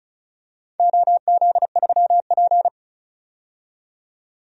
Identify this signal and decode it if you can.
Morse code, heard as a keyed tone, about 36 wpm, 690 Hz. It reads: OZ3P
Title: DXCC-100 Call Signs - 4 Characters - 35wpm